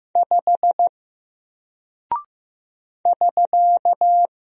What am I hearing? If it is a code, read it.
Morse code, heard as a keyed tone, about 15 wpm, 700 Hz. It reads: 5 <SK>
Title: Single Character (Letter:Number:Punctuation:Prosign) Alt 15wpm